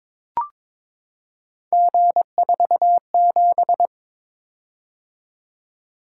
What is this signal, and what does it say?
Morse code, heard as a keyed tone, about 22 wpm, 700 Hz. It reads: G47
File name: practice / 22wpm / Three Letter-Number Combinations - 22wpm